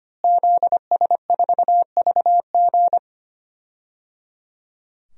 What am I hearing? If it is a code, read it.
Morse code, heard as a keyed tone, about 25 wpm, 700 Hz. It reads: ZS4VG
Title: DXCC-100 Call Signs - 5 Characters - Repeated 3x - 25wpm